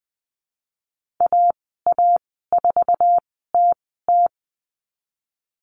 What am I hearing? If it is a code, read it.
Morse code, heard as a keyed tone, about 20 wpm, spaced as for 15 wpm, 700 Hz. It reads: AA4TT
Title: 076 Lesson 13g - Letter L - Call Signs 20wpm